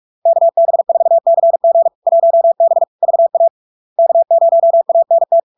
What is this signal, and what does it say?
Morse code, heard as a keyed tone, about 46 wpm, 670 Hz. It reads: KB4CG 1B VA K0ANT
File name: LARK-FD-2022-Log-in-CW-45wpm-670-Hz